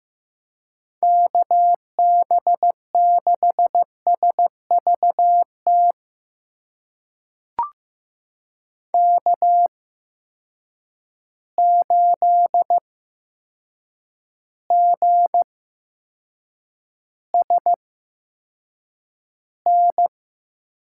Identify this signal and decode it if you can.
Morse code, heard as a keyed tone, about 15 wpm, spaced as for 7 wpm, 700 Hz. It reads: KB6SVT K8GSN
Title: USA Call Signs - 8x Character Spacing - 15wpm